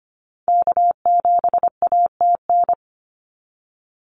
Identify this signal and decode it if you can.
Morse code, heard as a keyed tone, about 25 wpm, 700 Hz. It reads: K7ATN
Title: NA SOTA Activator Callsigns 25wpm